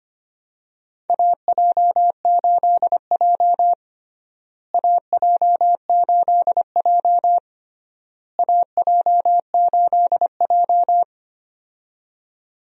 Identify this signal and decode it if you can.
Morse code, heard as a keyed tone, about 25 wpm, 700 Hz. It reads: AJ8J AJ8J AJ8J
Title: USA Call Signs - 4 Characters - Repeated 3x - 25wpm